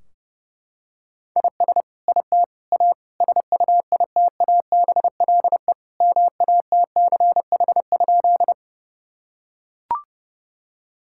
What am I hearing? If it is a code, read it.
Morse code, heard as a keyed tone, about 30 wpm, 700 Hz. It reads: IS IT A SUITABLE MATCH?
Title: Sentences from Top 1600 Words 30wpm